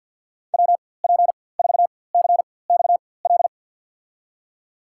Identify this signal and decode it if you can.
Morse code, heard as a keyed tone, about 51 wpm, spaced as for 23 wpm, 700 Hz. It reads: WP4CXL
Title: USA Call Signs - 6 Characters - 4x Character Spacing - 50wpm